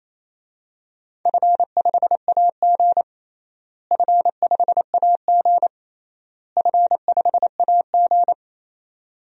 Morse code, heard as a keyed tone, about 28 wpm, 700 Hz. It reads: F5AG F5AG F5AG
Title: DXCC-100 Call Signs - 4 Characters - Repeated 3x - 28wpm